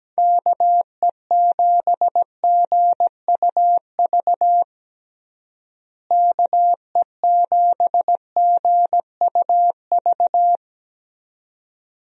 Morse code, heard as a keyed tone, about 17 wpm, 700 Hz. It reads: KE7GUV KE7GUV
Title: USA Call Signs - 6 Characters - Repeated 2x - 17wpm